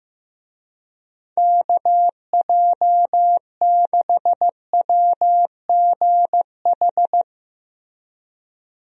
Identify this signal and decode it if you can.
Morse code, heard as a keyed tone, about 15 wpm, 700 Hz. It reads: KJ6WGH
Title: USA Call Signs - 6 Characters - Repeated 2x - 15wpm